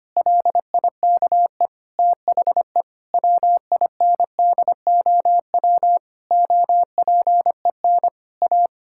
Morse code, heard as a keyed tone, about 25 wpm, 700 Hz. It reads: LIKE THE WINDOW OPEN A
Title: Sentences from Top 1600 Words 25wpm